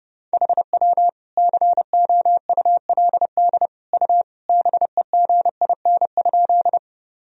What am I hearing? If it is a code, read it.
Morse code, heard as a keyed tone, about 30 wpm, 700 Hz. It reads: HW COULD U BEGIN?
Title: Sentences from Top 300 Words and Abbreviations 30wpm